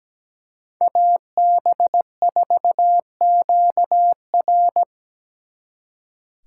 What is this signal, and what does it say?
Morse code, heard as a keyed tone, about 18 wpm, 700 Hz. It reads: AB4QR
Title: Speed Racing - USA Call Signs 17wpm